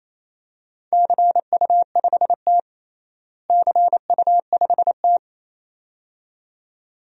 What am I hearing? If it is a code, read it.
Morse code, heard as a keyed tone, about 28 wpm, 700 Hz. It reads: CU5T CU5T
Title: DXCC-100 Call Signs - 4 Characters - Repeated 2x - 28wpm